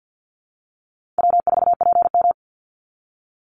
Morse code, heard as a keyed tone, about 50 wpm, 700 Hz. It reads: W4PM
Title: Easy Call Signs Repeated 2x - 50wpm